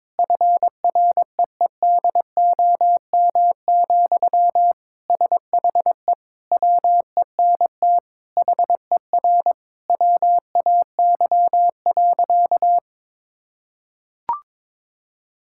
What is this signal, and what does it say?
Morse code, heard as a keyed tone, about 22 wpm, 700 Hz. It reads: FREEDOM, SHE WENT HER WAY.
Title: Sentences from Top 1000 Words - Review Part 5 - 22wpm